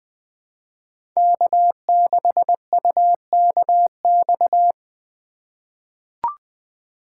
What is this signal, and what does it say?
Morse code, heard as a keyed tone, about 20 wpm, 700 Hz. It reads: K6UKX
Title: USA Call Signs - 5 Characters - 3x Character Spacing - 20wpm